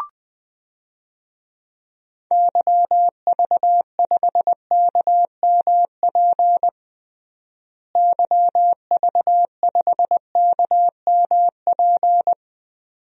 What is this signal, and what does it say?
Morse code, heard as a keyed tone, about 20 wpm, 700 Hz. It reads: YV5KMP YV5KMP
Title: DXCC-100 Call Signs - 6 Characters - Repeated 2x - 20wpm